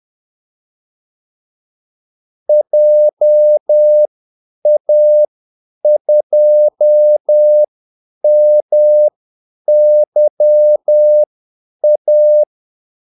Morse code, heard as a keyed tone, about 10 wpm, spaced as for 8 wpm, 600 Hz. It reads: JA2MYA
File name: Callsigns-10-8wpm